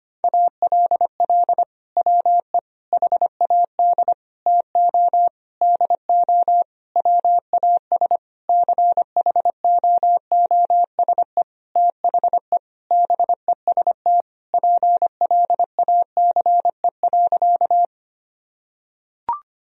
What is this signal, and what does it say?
Morse code, heard as a keyed tone, about 25 wpm, 700 Hz. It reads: ALL WE HAD TO DO WAS CHOOSE THE BEST PLACE.
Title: Sentences from Top 1000 Words - Review Part 4 - 25wpm